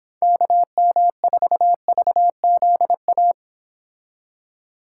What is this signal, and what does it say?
Morse code, heard as a keyed tone, about 26 wpm, 700 Hz. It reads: KM4VZA